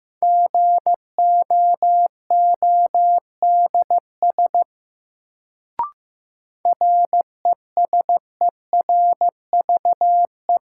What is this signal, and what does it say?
Morse code, heard as a keyed tone, about 15 wpm, 700 Hz. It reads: GOODS RESERVE
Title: Sentences from Top 1000 Words - Review Part 6 - New Word Drill - 15wpm